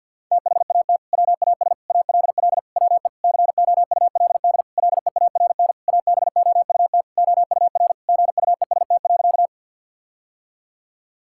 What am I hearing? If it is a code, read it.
Morse code, heard as a keyed tone, about 46 wpm, 700 Hz. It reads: THAT WAS ALL WE COULD LEARN ABOUT OUR GUEST.
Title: Sentences from Top 1600 Words 50wpm